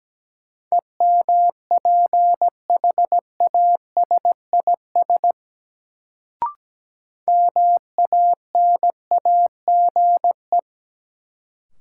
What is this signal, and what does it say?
Morse code, heard as a keyed tone, about 17 wpm, 700 Hz. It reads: EMPHASIS MANAGE
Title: Sentences from Top 1400 Words - New Word Drill - 17wpm